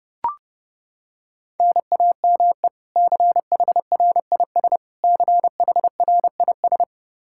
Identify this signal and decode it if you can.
Morse code, heard as a keyed tone, about 29 wpm, 700 Hz. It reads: NAME CHRIS CHRIS
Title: Practice QSO - Exchange 1 - Beginning - 30wpm